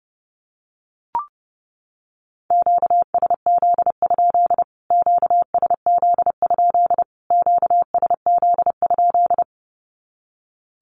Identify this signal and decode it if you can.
Morse code, heard as a keyed tone, about 30 wpm, 700 Hz. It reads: QSZ? QSZ? QSZ?
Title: Q-Codes Repeated 3x 30wpm